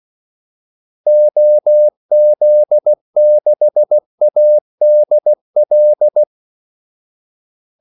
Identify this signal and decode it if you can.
Morse code, heard as a keyed tone, about 16 wpm, 600 Hz. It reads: OZ6ADL
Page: Kaldesignaler 16 wpm